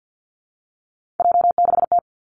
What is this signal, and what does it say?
Morse code, heard as a keyed tone, about 49 wpm, 700 Hz. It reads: J6T
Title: Three Challenging Letter-Number Combinations 50wpm